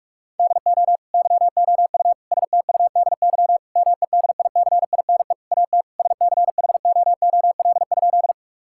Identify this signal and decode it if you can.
Morse code, heard as a keyed tone, about 46 wpm, 700 Hz. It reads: DO YOU STUDY MEDICINE AT SCHOOL?
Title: Sentences from Top 1000 Words - Review Part 6 - 45wpm